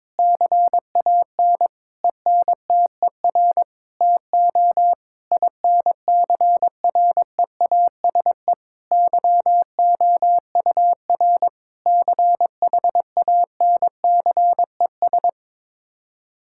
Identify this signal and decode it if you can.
Morse code, heard as a keyed tone, about 22 wpm, 700 Hz. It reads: CAN ENTER TO INCREASE YOUR CHANCES
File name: practice / 22wpm / Sets of 8 Words - 22wpm